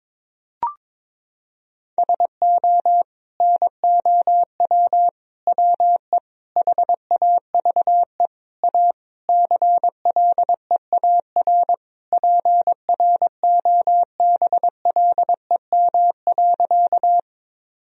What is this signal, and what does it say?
Morse code, heard as a keyed tone, about 22 wpm, 700 Hz. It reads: SO NOW WE HAVE A CLEAR PROBLEM.
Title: Sentences from Top 1000 Words - Review Part 2 - 22wpm